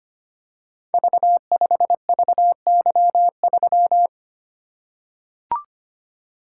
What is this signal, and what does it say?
Morse code, heard as a keyed tone, about 25 wpm, 700 Hz. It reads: V5VY3
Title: Code Groups - 5 Letter:Numbers - 25wpm